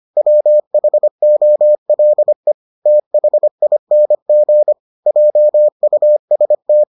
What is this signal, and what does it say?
Morse code, heard as a keyed tone, about 25 wpm, 600 Hz. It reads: WHOLE THING JUST